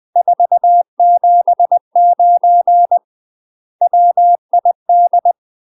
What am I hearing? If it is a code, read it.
Morse code, heard as a keyed tone, about 20 wpm, 700 Hz. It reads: 479 WID